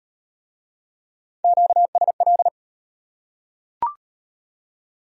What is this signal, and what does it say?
Morse code, heard as a keyed tone, about 37 wpm, 700 Hz. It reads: QSL